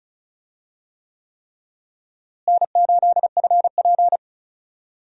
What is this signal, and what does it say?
Morse code, heard as a keyed tone, about 34 wpm, 700 Hz. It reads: N8FP
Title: USA Call Signs - 4 Characters - Repeated 3x - 35wpm